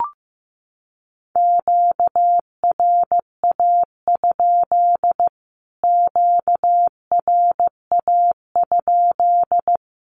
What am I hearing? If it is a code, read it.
Morse code, heard as a keyed tone, about 15 wpm, 700 Hz. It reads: QRA? QRA?
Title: Q-Codes Repeated 3x 15wpm